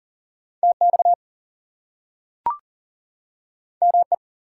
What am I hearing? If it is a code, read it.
Morse code, heard as a keyed tone, about 40 wpm, 700 Hz. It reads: TX ME
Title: US States and Canadian Provinces 40wpm